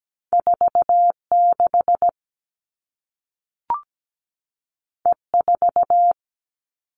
Morse code, heard as a keyed tone, about 17 wpm, 700 Hz. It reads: 46 E4
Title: All Cut Numbers - 2-Digit 17wpm